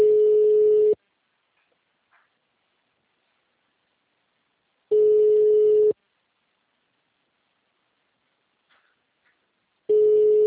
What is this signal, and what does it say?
Ringback tone (Europe, 425 Hz).